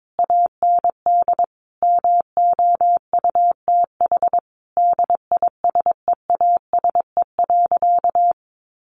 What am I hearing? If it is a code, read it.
Morse code, heard as a keyed tone, about 22 wpm, 700 Hz. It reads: AND MOUTH DISEASE.
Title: Sentences from Top 1500 Words 22wpm